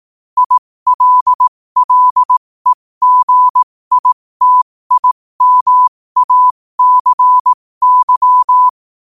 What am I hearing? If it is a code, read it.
Morse code, heard as a keyed tone, about 18 wpm, spaced as for 15 wpm, 1000 Hz. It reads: ILLEGITIMACY
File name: qotdfiles / 15wpm